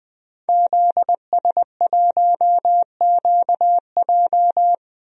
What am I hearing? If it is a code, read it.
Morse code, heard as a keyed tone, about 20 wpm, 700 Hz. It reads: ZS1QJ
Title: DXCC-100 Call Signs - 5 Characters - 20wpm